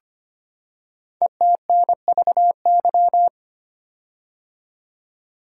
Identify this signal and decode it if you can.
Morse code, heard as a keyed tone, about 25 wpm, 700 Hz. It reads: ETNVY
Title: Code Groups - 5 Letters - 25wpm